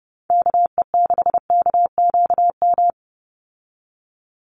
Morse code, heard as a keyed tone, about 30 wpm, 700 Hz. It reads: KE6KQM
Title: USA Call Signs 30wpm